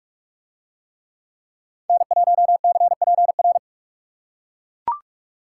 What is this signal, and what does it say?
Morse code, heard as a keyed tone, about 40 wpm, 700 Hz. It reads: N1CPR